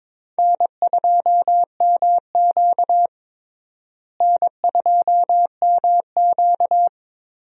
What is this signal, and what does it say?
Morse code, heard as a keyed tone, about 22 wpm, 700 Hz. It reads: N2MQ N2MQ
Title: USA Call Signs - 4 Characters - Repeated 3x - 22wpm